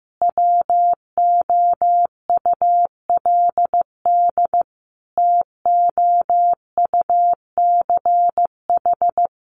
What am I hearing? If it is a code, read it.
Morse code, heard as a keyed tone, about 15 wpm, 700 Hz. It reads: WOULD TOUCH
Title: Sentences from Top 1200 Words 15wpm